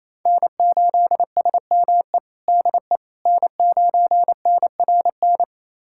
Morse code, heard as a keyed tone, about 28 wpm, 700 Hz. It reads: N8SME DE N9NRN
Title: USA Callsign DE Callsign - 5 Characters - 28wpm